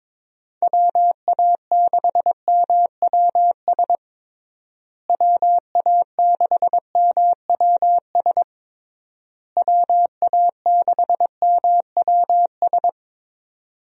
Morse code, heard as a keyed tone, about 22 wpm, 700 Hz. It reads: WA6MWS WA6MWS WA6MWS
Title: USA Call Signs - 6 Characters - Repeated 3x - 22wpm